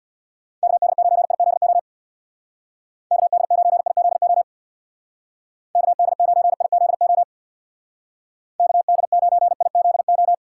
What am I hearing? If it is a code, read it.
Morse code, heard as a keyed tone, about 62 wpm, 700 Hz. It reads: KD9IZO KD9IZO KD9IZO KD9IZO
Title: Speed Racing - USA 2x3 Call Signs 50wpm